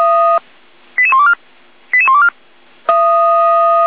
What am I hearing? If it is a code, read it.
Five-tone selective call (ZVEI), heard as quick runs of five tones, tones of about 70 ms. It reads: 88124 88124